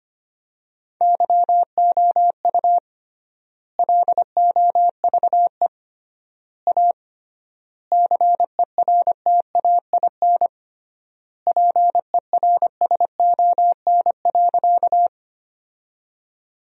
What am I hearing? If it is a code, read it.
Morse code, heard as a keyed tone, about 25 wpm, 700 Hz. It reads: YOU LOVE A CERTAIN PERSON.
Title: 3x Word-Spacing - Sentences from Top 500 Words - 25wpm